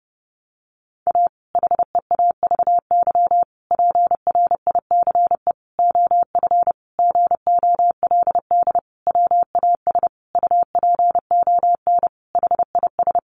Morse code, heard as a keyed tone, about 30 wpm, 700 Hz. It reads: A HEAVY PRICE OF GOLD WAS UPON HIS
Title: Sentences from Top 1100 Words 30wpm